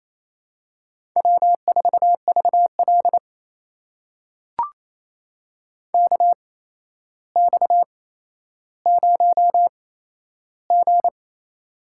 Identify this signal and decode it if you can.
Morse code, heard as a keyed tone, about 28 wpm, spaced as for 12 wpm, 700 Hz. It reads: W4VL KX0G
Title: USA Call Signs - 4 Characters - 8x Character Spacing - 28wpm